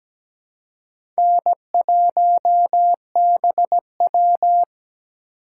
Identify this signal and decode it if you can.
Morse code, heard as a keyed tone, about 17 wpm, 700 Hz. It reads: N1BW
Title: USA Call Signs - 4 Characters - 2x Character Spacing - 17wpm